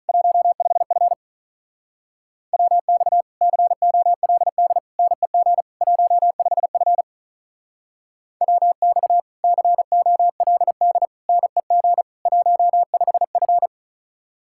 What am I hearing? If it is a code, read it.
Morse code, heard as a keyed tone, about 46 wpm, 700 Hz. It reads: 15F WX COLD NEG 15F WX COLD NEG 15F